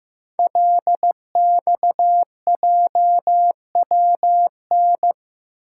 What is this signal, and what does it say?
Morse code, heard as a keyed tone, about 15 wpm, 700 Hz. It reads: LXJWN
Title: Code Groups - 5 Letter:Numbers - 15wpm